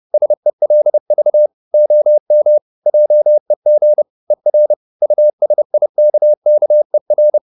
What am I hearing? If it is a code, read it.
Morse code, heard as a keyed tone, about 30 wpm, 600 Hz. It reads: SELV OM JEG ER USIKKER
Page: Ferie 30 wpm